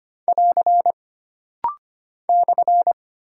Morse code, heard as a keyed tone, about 25 wpm, 700 Hz. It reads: <AR> /
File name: Single Punctuation-Prosign - Rapid-Fire - 25wpm